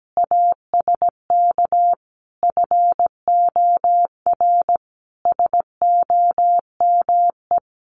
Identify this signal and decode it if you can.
Morse code, heard as a keyed tone, about 17 wpm, 700 Hz. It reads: ASK FOR SOME
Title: Sentences from Top 1500 Words 17wpm